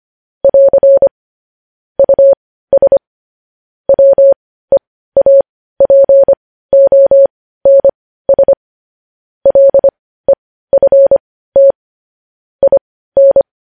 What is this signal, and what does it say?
Morse code, heard as a keyed tone, about 25 wpm, spaced as for 15 wpm, 550 Hz. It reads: <AR> US WEAPONS LEFT IN
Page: news-headlines-in-morse-code-at-15-wpm